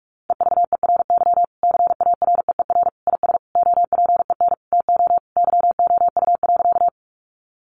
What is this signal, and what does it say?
Morse code, heard as a keyed tone, about 44 wpm, 700 Hz. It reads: EVERY CAREER IS OPEN TO YOU.
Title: Sentences from Top 1000 Words 45wpm